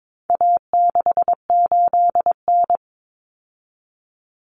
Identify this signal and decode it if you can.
Morse code, heard as a keyed tone, about 22 wpm, 700 Hz. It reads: A68N
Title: Common Cut Numbers - 4-Digit 22wpm